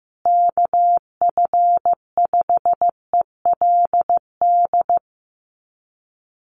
Morse code, heard as a keyed tone, about 15 wpm, 700 Hz. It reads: KF5ELD
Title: USA Call Signs Repeated 2x 15wpm